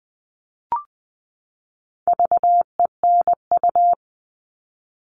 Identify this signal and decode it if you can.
Morse code, heard as a keyed tone, about 20 wpm, 700 Hz. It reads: VENU
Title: All Cut Numbers - 4-Digit 20wpm